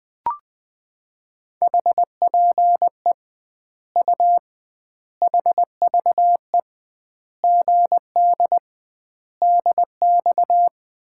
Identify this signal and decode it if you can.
Morse code, heard as a keyed tone, about 20 wpm, 700 Hz. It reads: HPE U HVE GD DX
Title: Common QSO Phrases - 2x Word-Spacing - 20wpm